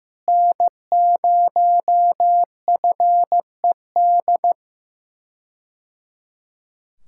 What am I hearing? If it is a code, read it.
Morse code, heard as a keyed tone, about 15 wpm, 700 Hz. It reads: N0FED
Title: K1USN SST QSO Practice - 15wpm